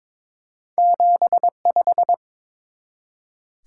Morse code, heard as a keyed tone, about 22 wpm, 700 Hz. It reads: 75
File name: practice / 22wpm / Two Letter-Number Combinations - 22wpm